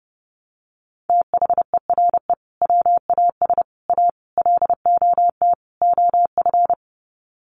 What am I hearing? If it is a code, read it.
Morse code, heard as a keyed tone, about 30 wpm, 700 Hz. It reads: THERE WAS A LOT OF
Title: Sets of 5 Words 30wpm